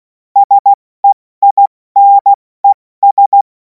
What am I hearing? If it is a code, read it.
Morse code, heard as a keyed tone, about 16 wpm, spaced as for 14 wpm, 800 Hz. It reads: SEINES
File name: Morsekurs / Witze / Witz 17. Februar 2026 70 F80